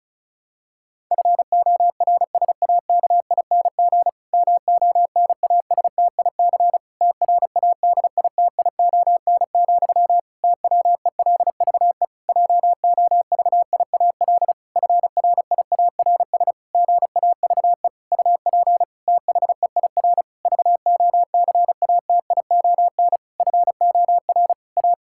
Morse code, heard as a keyed tone, about 36 wpm, 700 Hz. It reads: FORSAKING MONASTIC TRADITION, TWELVE JOVIAL FRIARS GAVE UP THEIR VOCATION FOR A
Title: Pangram Challenge v2 - 35wpm